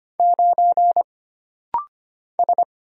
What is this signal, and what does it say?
Morse code, heard as a keyed tone, about 25 wpm, 700 Hz. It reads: 9 S
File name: Single Character (Letter:Number:Punctuation:Prosign) - Rapid-Fire - 25wpm